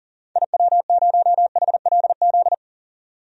Morse code, heard as a keyed tone, about 39 wpm, 700 Hz. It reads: IW0HLZ
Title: DXCC-100 Call Signs - 6 Characters - 40wpm